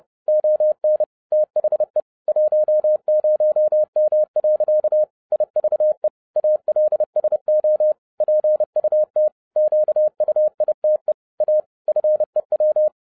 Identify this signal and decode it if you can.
Morse code, heard as a keyed tone, about 30 wpm, 610 Hz. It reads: ON THE 10M. IVE ALSO PUT QUITE A FEW